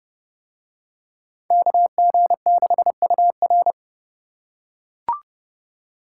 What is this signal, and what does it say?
Morse code, heard as a keyed tone, about 29 wpm, 700 Hz. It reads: KG6UR